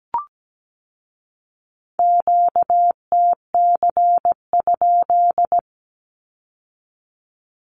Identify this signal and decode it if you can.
Morse code, heard as a keyed tone, about 17 wpm, 700 Hz. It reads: QTC?
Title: Q-Codes with Spoken Meaning - 17wpm